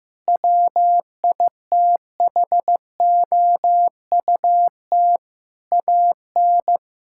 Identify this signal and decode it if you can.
Morse code, heard as a keyed tone, about 15 wpm, 700 Hz. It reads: WITHOUT AN
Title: Sentences from Top 1600 Words 15wpm